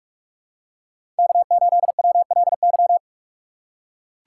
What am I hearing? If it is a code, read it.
Morse code, heard as a keyed tone, about 43 wpm, 700 Hz. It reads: K8WLY
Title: USA Call Signs - 5 Characters - 4x Character Spacing - 45wpm